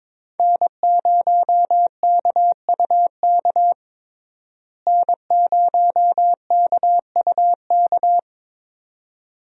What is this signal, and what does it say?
Morse code, heard as a keyed tone, about 22 wpm, 700 Hz. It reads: N0KUK N0KUK
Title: USA Call Signs - 5 Characters - Repeated 2x - 22wpm